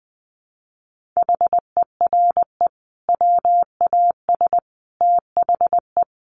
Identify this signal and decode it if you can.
Morse code, heard as a keyed tone, about 20 wpm, 700 Hz. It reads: HERE WAS THE
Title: Sentences from Top 1500 Words 20wpm